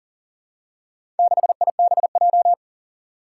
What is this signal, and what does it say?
Morse code, heard as a keyed tone, about 40 wpm, 700 Hz. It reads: 6IBJ
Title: Four Challenging Letter-Number Combinations 40wpm